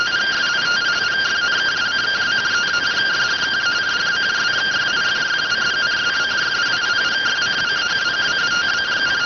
Two-tone teleprinter signal, about 170 Hz shift.